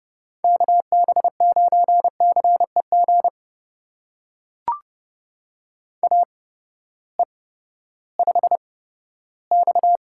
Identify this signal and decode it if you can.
Morse code, heard as a keyed tone, about 30 wpm, spaced as for 15 wpm, 700 Hz. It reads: KB9CEG AE5X
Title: USA Call Signs - 8x Character Spacing - 30wpm